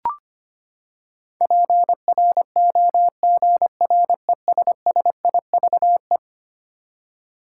Morse code, heard as a keyed tone, about 25 wpm, 700 Hz. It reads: PROGRESSIVE
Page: morse-code-ninja-practice-25wpm